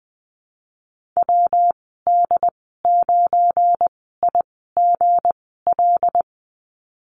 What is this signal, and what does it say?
Morse code, heard as a keyed tone, about 20 wpm, spaced as for 15 wpm, 700 Hz. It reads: WD9IGL
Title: 173 Lesson 28g - Number 9 - Call Signs 20wpm